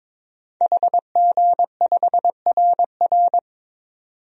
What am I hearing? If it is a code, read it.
Morse code, heard as a keyed tone, about 22 wpm, 700 Hz. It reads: HG5RR
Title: DXCC-100 Call Signs - 5 Characters - 22wpm